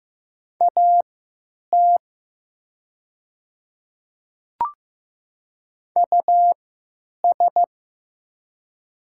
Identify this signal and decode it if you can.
Morse code, heard as a keyed tone, about 15 wpm, spaced as for 9 wpm, 700 Hz. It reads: AT US
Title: 3x Character Spacing - Top 300 Words - 15wpm